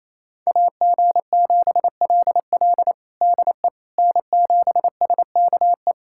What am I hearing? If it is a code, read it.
Morse code, heard as a keyed tone, about 28 wpm, 700 Hz. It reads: AG7LL DE N7SKE
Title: USA Callsign DE Callsign - 5 Characters - 28wpm